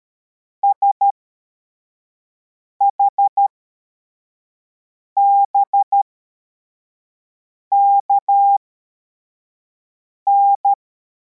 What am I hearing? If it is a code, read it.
Morse code, heard as a keyed tone, about 13 wpm, spaced as for 4 wpm, 800 Hz. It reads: SHBKN
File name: formation / cw / Text73